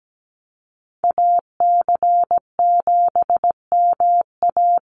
Morse code, heard as a keyed tone, about 17 wpm, 700 Hz. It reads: AC7MA
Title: NA SOTA Activator Callsigns 17wpm